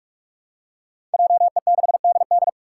Morse code, heard as a keyed tone, about 44 wpm, 700 Hz. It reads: JE6DD
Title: Five Challenging Letter-Number Combinations 45wpm